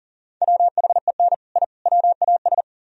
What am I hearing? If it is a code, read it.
Morse code, heard as a keyed tone, about 48 wpm, 700 Hz. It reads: WHEN I WAS
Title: Speed-Racing - Sets of 3 Words 40wpm